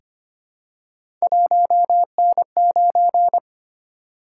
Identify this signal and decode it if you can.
Morse code, heard as a keyed tone, about 25 wpm, 700 Hz. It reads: 1N9
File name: Three Letter-Number Combinations - 25wpm